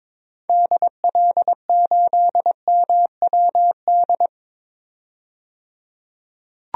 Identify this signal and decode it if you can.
Morse code, heard as a keyed tone, about 21 wpm, 700 Hz. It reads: DL8MWD
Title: DXCC-100 Call Signs - 6 Characters - 22wpm